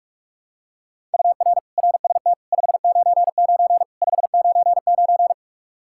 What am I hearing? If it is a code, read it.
Morse code, heard as a keyed tone, about 46 wpm, 700 Hz. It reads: UR RST 599 599
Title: Practice QSO - Exchange 1 - Beginning - 45wpm